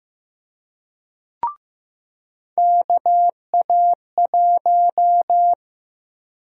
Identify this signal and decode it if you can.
Morse code, heard as a keyed tone, about 15 wpm, 700 Hz. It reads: KA1
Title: USA Call Sign Prefixes - Repeated 2x - 15wpm